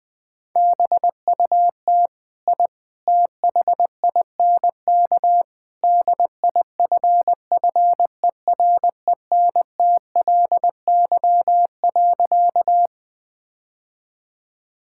Morse code, heard as a keyed tone, about 20 wpm, 700 Hz. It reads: BUT I THINK DIFFERENTLY.
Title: Sentences with 3-Word Phrases - Spoken Beginning - 20wpm